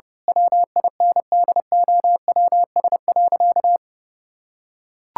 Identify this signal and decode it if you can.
Morse code, heard as a keyed tone, about 31 wpm, 700 Hz. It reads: WINDOWS.
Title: Pangram Challenge v2 - 30wpm